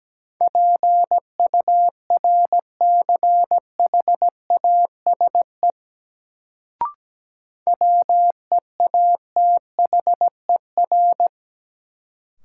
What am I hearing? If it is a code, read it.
Morse code, heard as a keyed tone, about 17 wpm, 700 Hz. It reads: PURCHASE WEATHER
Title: Sentences from Top 800 Words - New Word Drill - 17wpm